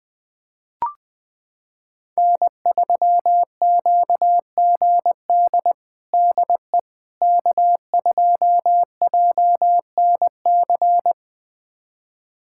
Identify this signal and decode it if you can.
Morse code, heard as a keyed tone, about 21 wpm, 700 Hz. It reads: N3QGD DE K2JNC
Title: USA Callsign DE Callsign - 5 Characters - 20wpm